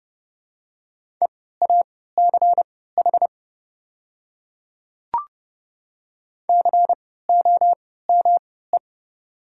Morse code, heard as a keyed tone, about 30 wpm, spaced as for 17 wpm, 700 Hz. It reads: EACH COME
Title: 3x Character Spacing - Top 100 Words - 30wpm